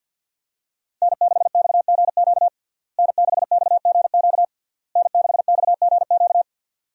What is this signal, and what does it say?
Morse code, heard as a keyed tone, about 51 wpm, 700 Hz. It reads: N6XGQ N6XGQ N6XGQ
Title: USA Call Signs - 5 Characters - Repeated 3x - 50wpm